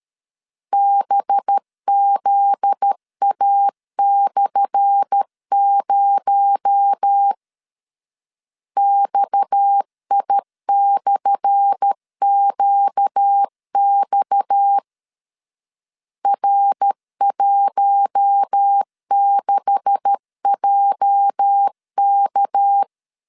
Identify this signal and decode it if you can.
Morse code, heard as a keyed tone, about 13 wpm, 800 Hz. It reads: BZA/0 XI/QX R16JK
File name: LEZIONI / 10 SIGNS / 10A SLASH 12wpm 2314